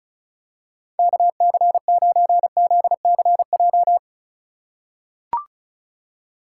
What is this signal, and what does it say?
Morse code, heard as a keyed tone, about 36 wpm, 700 Hz. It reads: KC9ZCJ
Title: USA Call Signs - 6 Characters - 8x Character Spacing - 35wpm